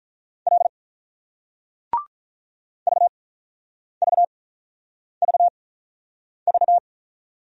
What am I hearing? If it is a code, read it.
Morse code, heard as a keyed tone, about 47 wpm, spaced as for 9 wpm, 700 Hz. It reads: L VVVV